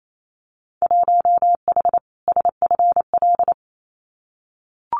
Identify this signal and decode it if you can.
Morse code, heard as a keyed tone, about 28 wpm, 700 Hz. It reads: 1H SFL
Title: Winter Field Day Exchanges 28wpm